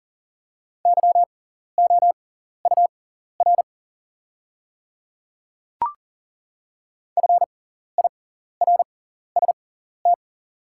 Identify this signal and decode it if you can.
Morse code, heard as a keyed tone, about 40 wpm, spaced as for 14 wpm, 700 Hz. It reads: YOUR FIRST
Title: 6x Character Spacing - Top 100 Words - 40wpm